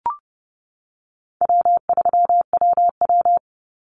Morse code, heard as a keyed tone, about 30 wpm, 700 Hz. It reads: W3WW
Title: Call Signs - 30wpm